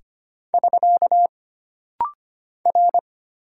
Morse code, heard as a keyed tone, about 25 wpm, 700 Hz. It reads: <SK> R
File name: Single Character (Letter:Number:Punctuation:Prosign) - Rapid-Fire - 25wpm